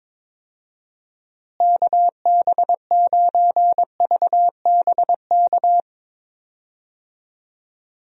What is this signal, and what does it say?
Morse code, heard as a keyed tone, about 22 wpm, 700 Hz. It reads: KB9VBK
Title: USA Call Signs - 6 Characters - Repeated 2x - 22wpm